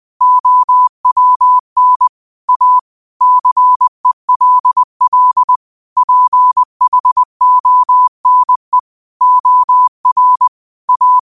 Morse code, heard as a keyed tone, about 20 wpm, 1000 Hz. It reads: OWN A CELL PHONE OR A
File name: qotdfiles / 20wpm